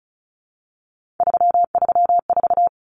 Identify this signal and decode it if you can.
Morse code, heard as a keyed tone, about 34 wpm, 700 Hz. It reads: 334
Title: Numbers - 3-Digit 35wpm